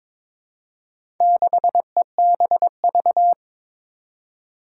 Morse code, heard as a keyed tone, about 22 wpm, 700 Hz. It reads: 6EBV
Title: Four Challenging Letter-Number Combinations 22wpm